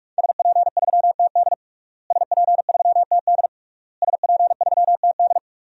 Morse code, heard as a keyed tone, about 46 wpm, 700 Hz. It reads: SP3TD SP3TD SP3TD
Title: DXCC-100 Call Signs - 5 Characters - Repeated 3x - 45wpm